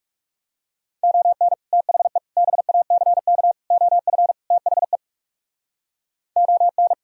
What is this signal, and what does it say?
Morse code, heard as a keyed tone, about 37 wpm, 700 Hz. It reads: ON THE BACK OF THE ON
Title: Speed-Racing - Sets of 5 Words 30wpm